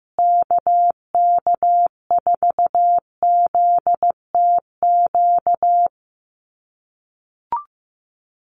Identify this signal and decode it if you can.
Morse code, heard as a keyed tone, about 15 wpm, 700 Hz. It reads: KK4ZTQ
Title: USA Call Signs Repeated 3x 15wpm